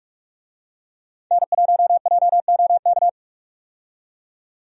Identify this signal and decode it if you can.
Morse code, heard as a keyed tone, about 46 wpm, 700 Hz. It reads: N1JOK
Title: USA Call Signs - 5 Characters - 45wpm